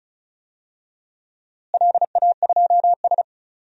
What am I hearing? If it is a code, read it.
Morse code, heard as a keyed tone, about 34 wpm, 700 Hz. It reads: LA2S
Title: DXCC-100 Call Signs - 4 Characters - Repeated 3x - 35wpm